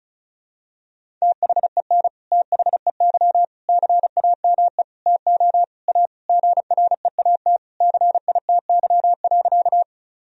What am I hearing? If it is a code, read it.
Morse code, heard as a keyed tone, about 36 wpm, 700 Hz. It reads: THEN THEY CAME TO A GREAT CITY.
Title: Sentences from Top 1000 Words - Review Part 1 - 35wpm